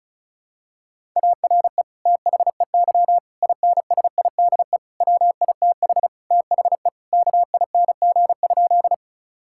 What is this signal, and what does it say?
Morse code, heard as a keyed tone, about 34 wpm, 700 Hz. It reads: ARE THEY INSIDE WITH THE KING?
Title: Sentences from Top 1000 Words - Review Part 6 - 35wpm